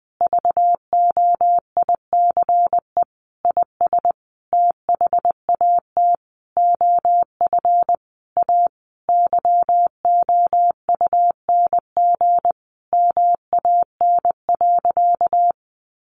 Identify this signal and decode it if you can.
Morse code, heard as a keyed tone, about 20 wpm, 700 Hz. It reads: VOICE IS THAT OF A YOUNG MAN.
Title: Sentences from Top 500 Words 20wpm